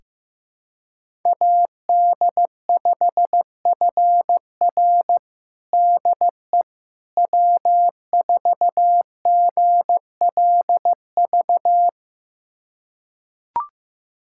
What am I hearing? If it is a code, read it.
Morse code, heard as a keyed tone, about 16 wpm, 700 Hz. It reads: AD5FR DE W4GLV
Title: USA Callsign DE Callsign - 5 Characters - 15wpm